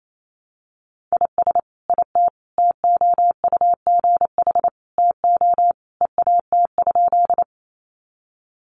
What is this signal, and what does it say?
Morse code, heard as a keyed tone, about 28 wpm, 700 Hz. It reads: IS IT TOUGH TO EAT?
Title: Sentences from Top 1200 Words 28wpm